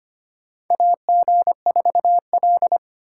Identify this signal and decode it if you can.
Morse code, heard as a keyed tone, about 25 wpm, 700 Hz. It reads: AG4L
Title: USA Call Signs - 8x Character Spacing - 25wpm